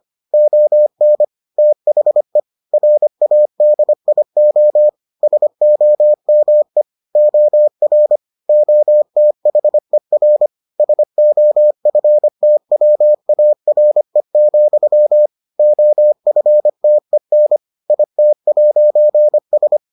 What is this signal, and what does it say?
Morse code, heard as a keyed tone, about 25 wpm, 600 Hz. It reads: ON THE RADIO SOME OR OTHER SOFTWARE, OFTEN IT'S